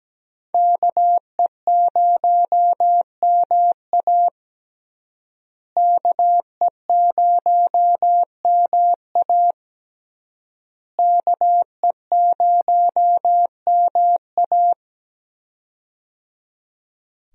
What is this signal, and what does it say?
Morse code, heard as a keyed tone, about 17 wpm, 700 Hz. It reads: KE0MA KE0MA KE0MA
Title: USA Call Signs - 5 Characters - Repeated 3x - 17wpm